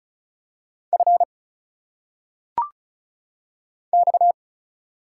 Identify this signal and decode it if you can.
Morse code, heard as a keyed tone, about 35 wpm, 700 Hz. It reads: F X
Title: Single Letters - Phonetic Pronunciation 35wpm